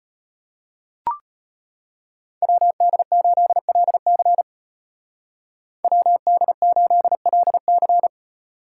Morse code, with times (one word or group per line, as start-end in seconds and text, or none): 2.4-4.4 WD8LC
5.8-8.1 WD8LC